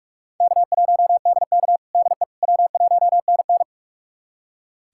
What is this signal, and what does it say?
Morse code, heard as a keyed tone, about 46 wpm, 700 Hz. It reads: K1DK DE W1NN
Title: USA Callsign DE Callsign - 4 Characters - 45wpm